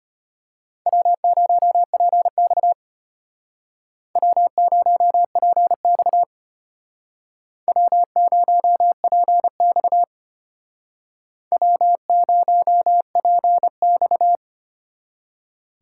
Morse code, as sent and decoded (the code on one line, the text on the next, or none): .-- ----- .--. -..- / .-- ----- .--. -..- / .-- ----- .--. -..- / .-- ----- .--. -..-
W0PX W0PX W0PX W0PX